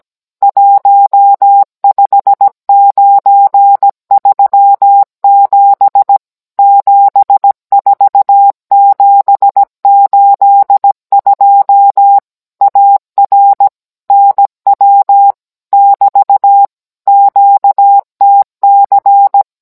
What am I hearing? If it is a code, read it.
Morse code, heard as a keyed tone, about 17 wpm, 800 Hz. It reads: 15937 74782 AR NW <BT> QTC